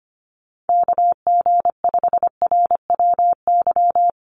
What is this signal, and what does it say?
Morse code, heard as a keyed tone, about 25 wpm, 700 Hz. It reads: KG5RWY
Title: USA Call Signs 25wpm